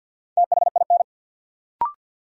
Morse code, heard as a keyed tone, about 52 wpm, 700 Hz. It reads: THIN
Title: Sentences from Top 1200 Words - New Word Drill Alt - 50wpm